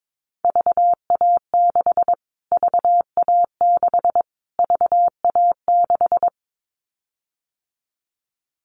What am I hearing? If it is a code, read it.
Morse code, heard as a keyed tone, about 22 wpm, 700 Hz. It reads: VA6 VA6 VA6
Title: Callsign Prefixes for Top 25 DXCC - Repeated 3x - 22wpm